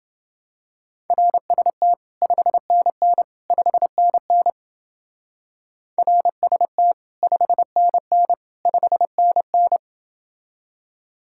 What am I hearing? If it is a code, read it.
Morse code, heard as a keyed tone, about 29 wpm, 700 Hz. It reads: RST 5NN 5NN RST 5NN 5NN